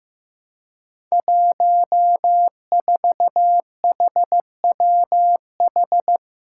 Morse code, heard as a keyed tone, about 15 wpm, 700 Hz. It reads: 14HWH
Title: Five Challenging Letter-Number Combinations 15wpm